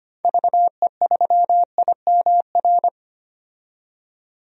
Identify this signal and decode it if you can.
Morse code, heard as a keyed tone, about 25 wpm, 700 Hz. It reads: VE3IMR
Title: DXCC-100 Call Signs - 6 Characters - 25wpm